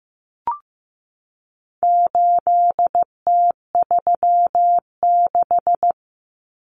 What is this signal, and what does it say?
Morse code, heard as a keyed tone, about 15 wpm, 700 Hz. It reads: 8T36
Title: Common Cut Numbers - 4-Digit 15wpm